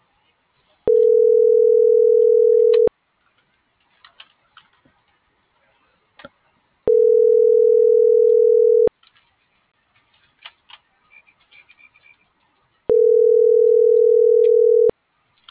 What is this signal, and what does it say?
Ringback tone (North America), 440 + 480 Hz.
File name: CENTRAL TERRESTRE / out cris.f. - .48430